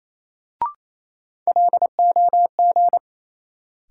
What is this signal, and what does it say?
Morse code, heard as a keyed tone, about 28 wpm, 700 Hz. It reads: LOG
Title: Sentences from Top 1400 Words - New Word Drill Alt - 28wpm